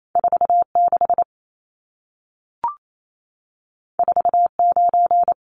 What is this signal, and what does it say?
Morse code, heard as a keyed tone, about 28 wpm, 700 Hz. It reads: 46 49
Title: Numbers - 2-Digit 28wpm